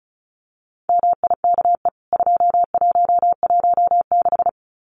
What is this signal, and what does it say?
Morse code, heard as a keyed tone, about 34 wpm, 700 Hz. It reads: MIKE 2116
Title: CWT Exchange 35wpm